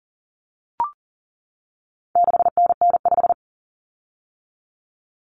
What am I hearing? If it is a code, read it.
Morse code, heard as a keyed tone, about 40 wpm, 700 Hz. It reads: 6NN5
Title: Common Cut Numbers - 4-Digit 40wpm